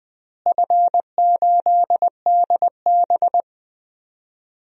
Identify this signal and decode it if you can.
Morse code, heard as a keyed tone, about 20 wpm, 700 Hz. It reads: F8DB
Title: DXCC-100 Call Signs - 4 Characters - Repeated 3x - 20wpm